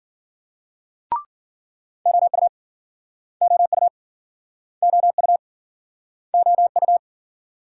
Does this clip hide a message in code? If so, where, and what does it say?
Yes: Morse code from 2.1 to 7.0 s, heard as a keyed tone, about 53 wpm, 700 Hz. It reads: OU OU OU OU